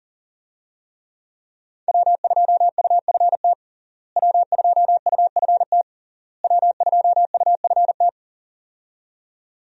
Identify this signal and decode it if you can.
Morse code, heard as a keyed tone, about 40 wpm, 700 Hz. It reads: W2UFT W2UFT W2UFT
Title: USA Call Signs - 5 Characters - Repeated 3x - 40wpm